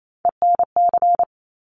Morse code, heard as a keyed tone, about 28 wpm, 700 Hz. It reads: ENC
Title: Top Three Letter Combinations in Words 28wpm